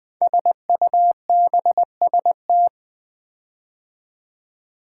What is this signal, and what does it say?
Morse code, heard as a keyed tone, about 20 wpm, 700 Hz. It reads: SUBST
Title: Build A Word - 5 Letters At A Time - 12-Letter Words - 20wpm